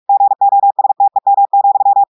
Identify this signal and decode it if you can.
Morse code, heard as a keyed tone, about 46 wpm, 800 Hz. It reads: GOSTEM,
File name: audios / portugais / portugais texte en clair 2 vitesse 45 mots minutes